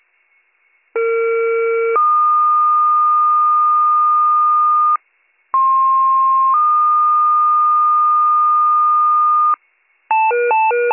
Two-tone page, heard as a one-second tone, then a three-second tone, about 460 then 1220 Hz, about 1060 then 1220 Hz. It is heard from 1.0 to 9.6 s.